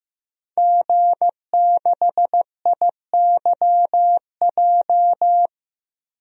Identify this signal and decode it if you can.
Morse code, heard as a keyed tone, about 15 wpm, 700 Hz. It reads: G6IYJ
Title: DXCC-100 Call Signs - 5 Characters - Repeated 3x - 15wpm